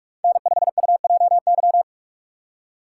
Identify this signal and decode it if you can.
Morse code, heard as a keyed tone, about 40 wpm, 700 Hz. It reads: N5UJY